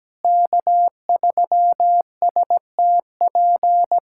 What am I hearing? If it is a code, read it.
Morse code, heard as a keyed tone, about 17 wpm, 700 Hz. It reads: K3STP
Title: USA Callsign DE Callsign - 5 Characters - 17wpm